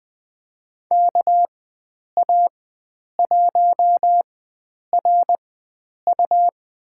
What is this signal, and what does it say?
Morse code, heard as a keyed tone, about 20 wpm, spaced as for 9 wpm, 700 Hz. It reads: KA1RU
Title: USA Call Signs - 4x Character Spacing - 20wpm